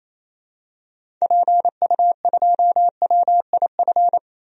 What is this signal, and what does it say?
Morse code, heard as a keyed tone, about 28 wpm, 700 Hz. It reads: PU2WIF
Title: DXCC-100 Call Signs - 6 Characters - 28wpm